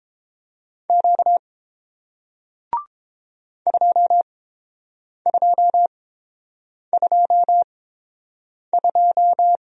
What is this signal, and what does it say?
Morse code, heard as a keyed tone, about 30 wpm, spaced as for 7 wpm, 700 Hz. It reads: Q 2222